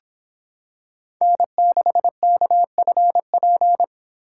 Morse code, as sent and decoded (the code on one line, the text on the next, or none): -. -.... -.- ..-. .--.
N6KFP